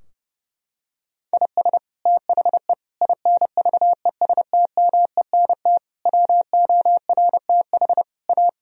Morse code, heard as a keyed tone, about 30 wpm, 700 Hz. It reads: IS THE INVESTMENT WORTH A
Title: Sentences from Top 1000 Words - Review Part 4 - 30wpm